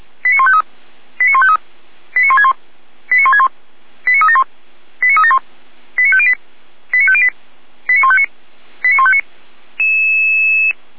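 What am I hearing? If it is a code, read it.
Five-tone selective call (ZVEI), heard as quick runs of five tones, tones of about 70 ms. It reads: 89153 89153 89161 89161 89371 89371 89508 89508 89159 89159
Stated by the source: ZVEI five-tone selective call received on an SDR, from the Feuerwehr Orlamünde server.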